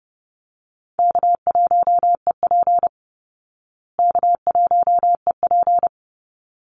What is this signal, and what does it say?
Morse code, heard as a keyed tone, about 30 wpm, 700 Hz. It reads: K1EP K1EP
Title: Easy Call Signs Repeated 2x - 30wpm